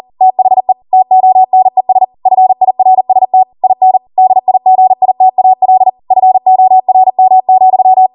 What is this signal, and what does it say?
Morse code, heard as a keyed tone, about 40 wpm, 750 Hz. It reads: THE TONES FIRST IN DIGITAL FORM,